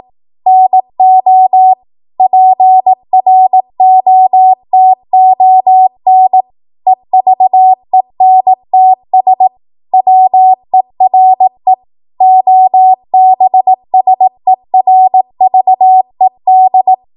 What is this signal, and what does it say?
Morse code, heard as a keyed tone, about 18 wpm, 750 Hz. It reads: NO PROTON EVENTS WERE OBSERVED
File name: Morse / 18WPM